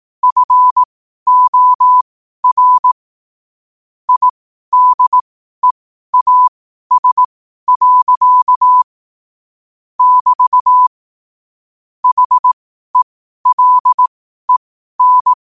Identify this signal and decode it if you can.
Morse code, heard as a keyed tone, about 18 wpm, spaced as for 12 wpm, 1000 Hz. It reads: FOR IDEAS. <BT> HELEN
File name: qotdfiles / 12wpm